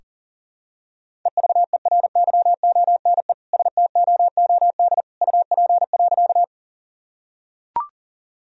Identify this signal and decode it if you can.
Morse code, heard as a keyed tone, about 38 wpm, 700 Hz. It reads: EVERYONE STOOD UP.
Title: Sentences from Top 1000 Words - Review Part 2 - 40wpm